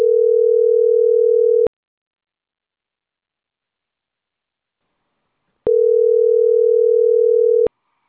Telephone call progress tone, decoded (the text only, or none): ringback tone